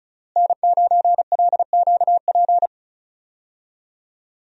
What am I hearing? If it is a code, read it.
Morse code, heard as a keyed tone, about 34 wpm, 700 Hz. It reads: N9LQP
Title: USA Call Signs - 5 Characters - Repeated 3x - 35wpm